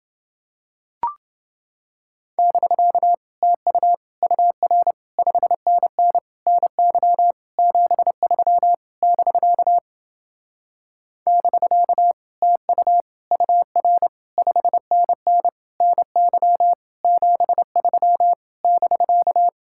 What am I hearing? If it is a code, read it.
Morse code, heard as a keyed tone, about 29 wpm, 700 Hz. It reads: <BK> TU UR 5NN NY 73 <BK> <BK> TU UR 5NN NY 73 <BK>